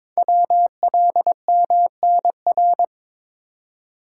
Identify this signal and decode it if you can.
Morse code, heard as a keyed tone, about 23 wpm, 700 Hz. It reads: WLMNR
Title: Code Groups - 5 Letters - 22wpm